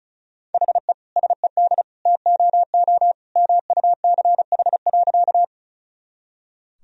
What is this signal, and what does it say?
Morse code, heard as a keyed tone, about 36 wpm, 700 Hz. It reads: HE SED TOO MUCH.
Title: Sentences from Top 200 Words and Abbreviations 35wpm